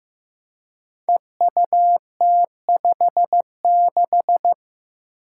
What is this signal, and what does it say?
Morse code, heard as a keyed tone, about 15 wpm, 700 Hz. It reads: EUT56
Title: Code Groups - 5 Letter:Numbers - 15wpm